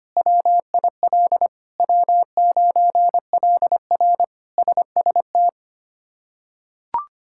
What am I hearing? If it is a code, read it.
Morse code, heard as a keyed tone, about 25 wpm, 700 Hz. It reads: WIL W9LR SST
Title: K1USN SST QSO Practice - 25wpm